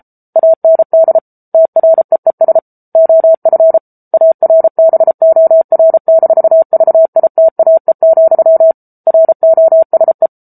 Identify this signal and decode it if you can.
Morse code, heard as a keyed tone, about 34 wpm, 650 Hz. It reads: AND TREES OF ARBOR-VITAE, ROSE
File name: Book Two / 07-Chapter 7 - 33 WPM